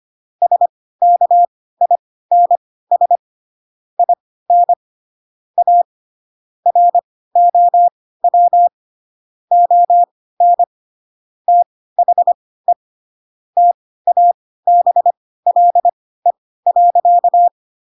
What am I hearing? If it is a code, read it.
Morse code, heard as a keyed tone, about 25 wpm, spaced as for 16 wpm, 700 Hz. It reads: SKINS IN A ROW ON THE TABLE.
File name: Pinocchio / Pinocchio ch7 10 25 ik1hge